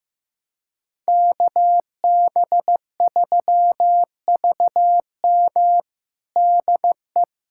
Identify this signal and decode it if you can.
Morse code, heard as a keyed tone, about 15 wpm, 700 Hz. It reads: KB3VM DE
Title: USA Callsign DE Callsign - 5 Characters - 15wpm